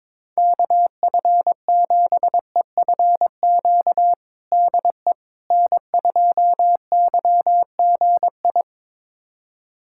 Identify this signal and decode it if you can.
Morse code, heard as a keyed tone, about 21 wpm, 700 Hz. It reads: KF7EFQ DE N2YGI
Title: USA Callsign DE Callsign 22wpm